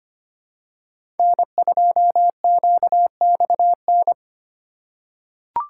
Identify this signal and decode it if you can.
Morse code, heard as a keyed tone, about 25 wpm, 700 Hz. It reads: N2QXN
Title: USA Call Signs - 5 Characters - 4x Character Spacing - 25wpm